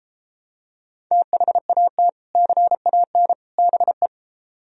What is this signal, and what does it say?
Morse code, heard as a keyed tone, about 34 wpm, 700 Hz. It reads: THAT CAN BE